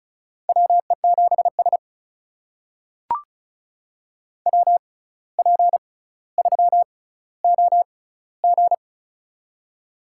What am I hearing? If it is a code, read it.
Morse code, heard as a keyed tone, about 34 wpm, spaced as for 17 wpm, 700 Hz. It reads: WE7S WP3OG
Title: USA Call Signs - 6x Character Spacing - 35wpm